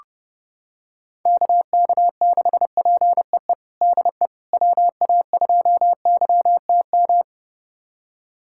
Morse code, heard as a keyed tone, about 32 wpm, 700 Hz. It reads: KK6PEE DE WA2YTM
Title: USA Callsign DE Callsign - 6 Characters - 30wpm